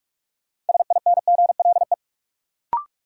Morse code, heard as a keyed tone, about 46 wpm, 700 Hz. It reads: SINGLE
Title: Sentences from Top 700 Words - New Word Drill - 45wpm